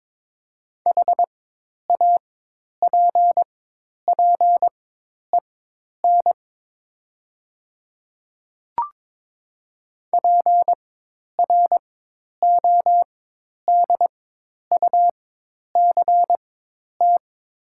Morse code, heard as a keyed tone, about 22 wpm, spaced as for 10 wpm, 700 Hz. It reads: HAPPEN PRODUCT
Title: 4x Character Spacing - Top 500 Words - 22wpm